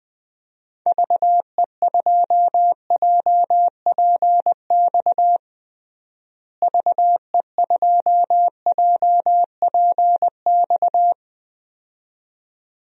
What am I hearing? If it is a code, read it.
Morse code, heard as a keyed tone, about 20 wpm, 700 Hz. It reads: VE2JPX VE2JPX
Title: DXCC-100 Call Signs - 6 Characters - Repeated 2x - 20wpm